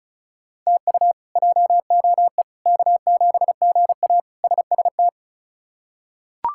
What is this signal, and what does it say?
Morse code, heard as a keyed tone, about 34 wpm, 700 Hz. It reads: TU JOE K7GA SST
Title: K1USN SST QSO Practice - 35wpm